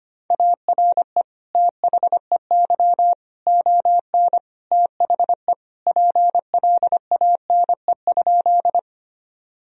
Morse code, heard as a keyed tone, about 25 wpm, 700 Hz. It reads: ARE THEY ON THE PLANE?
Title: Sentences from Top 1000 Words - Review Part 3 - 25wpm